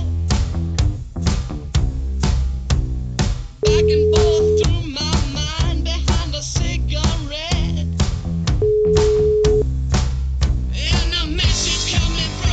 Ringback tone (Europe, 425 Hz).